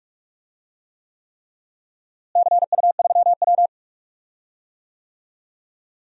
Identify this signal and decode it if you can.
Morse code, heard as a keyed tone, about 47 wpm, 700 Hz. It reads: CU3W
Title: DXCC-100 Call Signs - 4 Characters - Repeated 3x - 45wpm